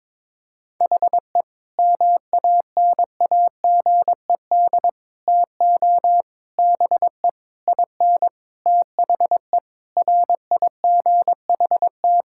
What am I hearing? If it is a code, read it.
Morse code, heard as a keyed tone, about 21 wpm, 700 Hz. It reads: HE MANAGED TO BE IN THE RIGHT
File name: Sentences with 9-Word Phrases - 22wpm